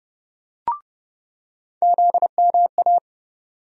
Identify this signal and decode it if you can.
Morse code, heard as a keyed tone, about 30 wpm, 700 Hz. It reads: ZMA
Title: USA Call Sign Postfixes - 3 Letters 30wpm